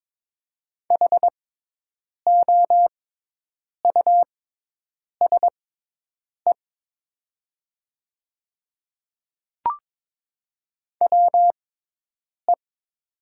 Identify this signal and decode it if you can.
Morse code, heard as a keyed tone, about 22 wpm, spaced as for 8 wpm, 700 Hz. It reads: HOUSE WE
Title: 6x Character Spacing - Top 200 Words - 22wpm